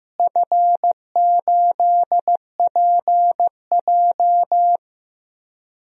Morse code, heard as a keyed tone, about 15 wpm, 700 Hz. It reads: F8PJ
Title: DXCC-100 Call Signs - 4 Characters - Repeated 2x - 15wpm